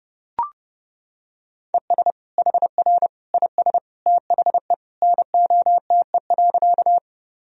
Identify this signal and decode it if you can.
Morse code, heard as a keyed tone, about 32 wpm, 700 Hz. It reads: ES HR IS THE NOTE.
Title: Sentences from Top 500 Words and Abbreviations 30wpm